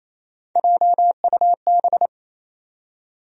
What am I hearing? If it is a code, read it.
Morse code, heard as a keyed tone, about 28 wpm, 700 Hz. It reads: JUB
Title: USA Call Sign Postfixes - 3 Letters 28wpm